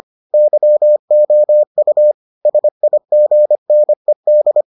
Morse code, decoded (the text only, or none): YOU SIGNED